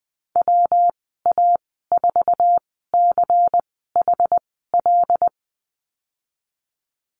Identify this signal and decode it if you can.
Morse code, heard as a keyed tone, about 20 wpm, spaced as for 14 wpm, 700 Hz. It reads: WA4CHL
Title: 128 Lesson 21g - Number 6 - Call Signs 20wpm